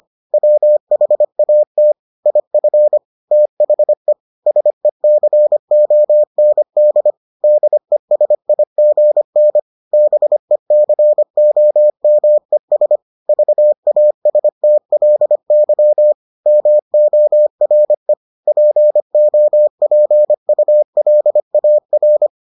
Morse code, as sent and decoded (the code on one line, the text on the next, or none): .-- .... .- - / .. ..-. / - .... / ... -.-. --- -. -.. / -.. ... .. --. -. / -... -.-. --- -- . ... / ...- .- ... - .-.. -.-- / -- --- .-. / .--. --- .--. ..- .-.. .- .-.
WHAT IF THE SECOND DESIGN BECOMES VASTLY MORE POPULAR